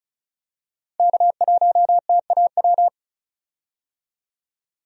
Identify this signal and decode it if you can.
Morse code, heard as a keyed tone, about 36 wpm, 700 Hz. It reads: K1TAW
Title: USA Call Signs - 5 Characters - 35wpm